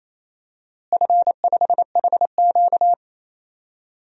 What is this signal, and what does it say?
Morse code, heard as a keyed tone, about 28 wpm, 700 Hz. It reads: F5HQ
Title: DXCC-100 Call Signs - 4 Characters - Repeated 2x - 28wpm